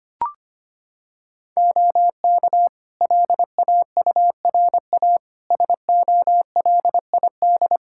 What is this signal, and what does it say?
Morse code, heard as a keyed tone, about 25 wpm, 700 Hz. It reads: OK LAURA SOLID
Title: K1USN SST QSO Practice 25wpm